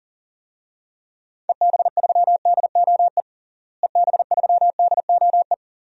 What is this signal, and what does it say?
Morse code, heard as a keyed tone, about 37 wpm, 700 Hz. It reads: EB3DOE EB3DOE
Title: DXCC-100 Call Signs - 6 Characters - Repeated 3x - 40wpm